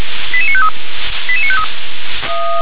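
Five-tone selective call (ZVEI), heard as quick runs of five tones, tones of about 70 ms. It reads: 88953 88953
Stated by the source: ZVEI five-tone call 88922 received on SDR.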